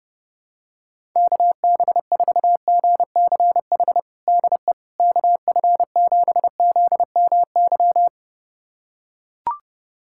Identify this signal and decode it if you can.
Morse code, heard as a keyed tone, about 30 wpm, 700 Hz. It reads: KB4GCH DE KF7ZMY
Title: USA Callsign DE Callsign 30wpm